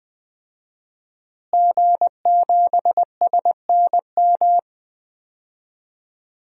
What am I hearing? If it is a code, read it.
Morse code, heard as a keyed tone, about 20 wpm, 700 Hz. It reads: G7SNM
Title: DXCC-100 Call Signs - 5 Characters - Repeated 2x - 20wpm